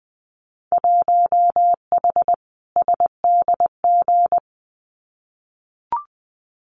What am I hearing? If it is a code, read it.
Morse code, heard as a keyed tone, about 20 wpm, 700 Hz. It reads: 1H SDG
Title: Winter Field Day Exchanges 20wpm